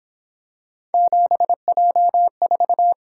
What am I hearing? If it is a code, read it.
Morse code, heard as a keyed tone, about 27 wpm, 700 Hz. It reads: 7J4
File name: Speed-Racing - Three Letter-Number Combinations - 22wpm